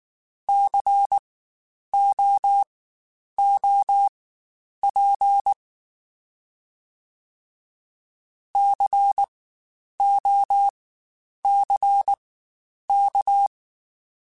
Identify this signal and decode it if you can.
Morse code, heard as a keyed tone, about 19 wpm, spaced as for 9 wpm, 800 Hz. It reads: COOP COCK
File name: Spaced Alphas / Spaced Alphas mp3s / Cs sp @18